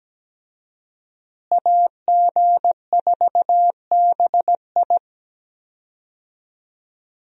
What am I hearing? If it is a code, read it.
Morse code, heard as a keyed tone, about 17 wpm, 700 Hz. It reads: AG4BI
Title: USA Call Signs - 5 Characters - Repeated 3x - 17wpm